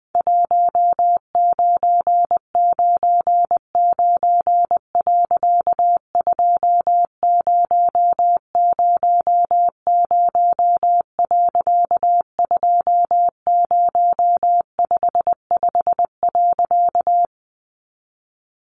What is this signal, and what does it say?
Morse code, heard as a keyed tone, about 20 wpm, 700 Hz. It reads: 1999. 2000. 2055.
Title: A Sound Of Thunder 20wpm